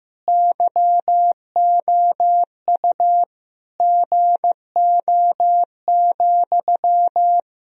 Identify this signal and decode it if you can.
Morse code, heard as a keyed tone, about 15 wpm, 700 Hz. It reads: YOU GO,
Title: Sentences from Top 1000 Words - Review Part 1 - 15wpm